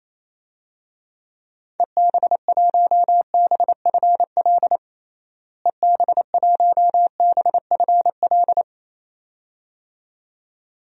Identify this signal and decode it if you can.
Morse code, heard as a keyed tone, about 28 wpm, 700 Hz. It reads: EB1BFL EB1BFL
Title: DXCC-100 Call Signs - 6 Characters - Repeated 2x - 28wpm